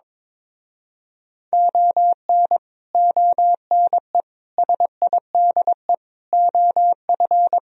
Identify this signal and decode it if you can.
Morse code, heard as a keyed tone, about 22 wpm, 700 Hz. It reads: ON ONE SIDE OF
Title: Speed-Racing - Sets of 5 Words 22wpm